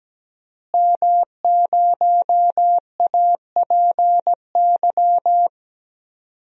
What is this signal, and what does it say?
Morse code, heard as a keyed tone, about 17 wpm, 700 Hz. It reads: M0APY
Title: DXCC-100 Call Signs - 5 Characters - Repeated 2x - 17wpm